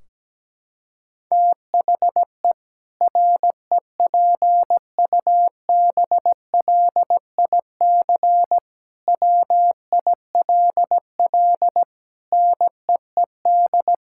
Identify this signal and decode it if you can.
Morse code, heard as a keyed tone, about 17 wpm, 700 Hz. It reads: THE REPUBLIC WILL NEED
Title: Sentences from Top 1600 Words 17wpm